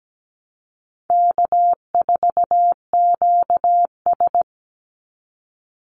Morse code, heard as a keyed tone, about 17 wpm, 700 Hz. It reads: K4QS
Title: NA SOTA Chaser Callsigns 17wpm